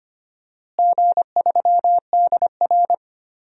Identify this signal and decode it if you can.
Morse code, heard as a keyed tone, about 25 wpm, 700 Hz. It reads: G3DR
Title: DXCC-100 Call Signs - 4 Characters - Repeated 3x - 25wpm